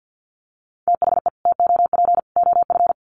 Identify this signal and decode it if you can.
Morse code, heard as a keyed tone, about 49 wpm, 700 Hz. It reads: THE TOP OF
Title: Sets of 3 Words 50wpm